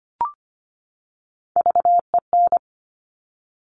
Morse code, heard as a keyed tone, about 25 wpm, 700 Hz. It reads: VEN
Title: Top Three Letter Combinations in Words 25wpm